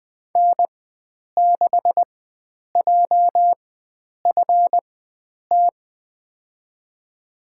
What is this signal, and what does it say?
Morse code, heard as a keyed tone, about 20 wpm, spaced as for 9 wpm, 700 Hz. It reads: N6JFT
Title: USA Call Signs - 5 Characters - 4x Character Spacing - 20wpm